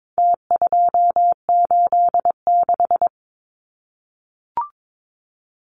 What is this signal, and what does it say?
Morse code, heard as a keyed tone, about 22 wpm, 700 Hz. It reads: T286
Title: Numbers - Contest Style - 4-Digit 22wpm